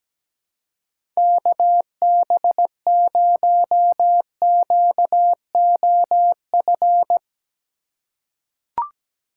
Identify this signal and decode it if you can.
Morse code, heard as a keyed tone, about 17 wpm, 700 Hz. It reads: KB0QOF
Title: USA Call Signs - 6 Characters - 4x Character Spacing - 17wpm